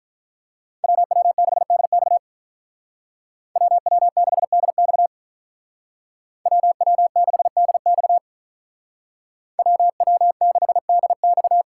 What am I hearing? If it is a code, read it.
Morse code, heard as a keyed tone, about 47 wpm, 700 Hz. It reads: WW6DX WW6DX WW6DX WW6DX